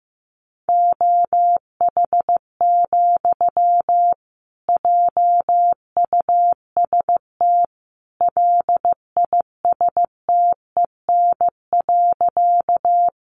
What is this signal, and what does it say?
Morse code, heard as a keyed tone, about 15 wpm, 700 Hz. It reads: OH, JUST LISTEN.
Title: Sentences from Top 500 Words 15wpm